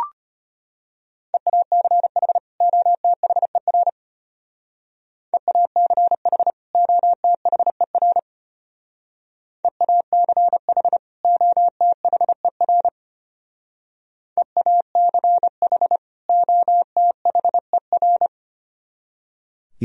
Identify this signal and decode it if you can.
Morse code, heard as a keyed tone, about 30 wpm, 700 Hz. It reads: EACH OTHER EACH OTHER EACH OTHER EACH OTHER
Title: Speed-Racing - Sets of 2 Words 25wpm